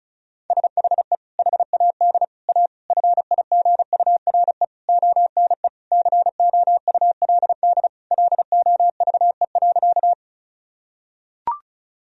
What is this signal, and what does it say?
Morse code, heard as a keyed tone, about 36 wpm, 700 Hz. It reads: SHE HAD A FIGURE ONE COULD LOVE.
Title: Sentences from Top 1000 Words - Review Part 2 - 35wpm